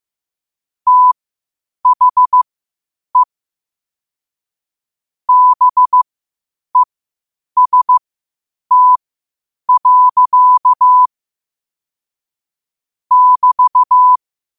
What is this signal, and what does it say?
Morse code, heard as a keyed tone, about 15 wpm, spaced as for 8 wpm, 1000 Hz. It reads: THE BEST. <BT>
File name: qotdfiles / 8wpm